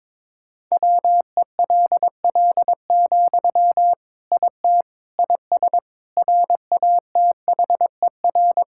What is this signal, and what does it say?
Morse code, heard as a keyed tone, about 22 wpm, 700 Hz. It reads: WELL, IT IS RATHER
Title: Sentences from Top 1600 Words 22wpm